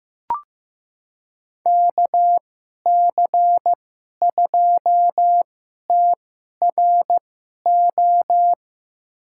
Morse code, heard as a keyed tone, about 15 wpm, spaced as for 11 wpm, 700 Hz. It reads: KC2TRO
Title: USA Call Signs - 6 Characters - 2x Character Spacing - 15wpm